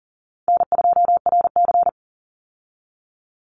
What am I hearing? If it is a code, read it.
Morse code, heard as a keyed tone, about 39 wpm, 700 Hz. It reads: N2RC
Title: Easy Call Signs Repeated 2x - 40wpm